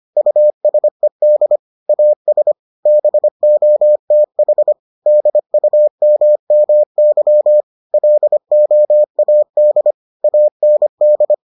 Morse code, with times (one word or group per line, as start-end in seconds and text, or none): 0.2-1.6 USED
1.9-2.5 AS
2.9-4.7 BOTH
5.1-7.6 DUMMY
7.9-9.9 LOAD
10.2-11.3 AND